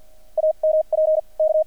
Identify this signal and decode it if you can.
Morse code, heard as a keyed tone, about 50 wpm, spaced as for 41 wpm, 620 Hz. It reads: AMP K
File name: archived / CRSnet